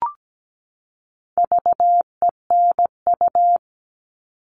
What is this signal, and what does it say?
Morse code, heard as a keyed tone, about 17 wpm, 700 Hz. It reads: VENU
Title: All Cut Numbers - 4-Digit 17wpm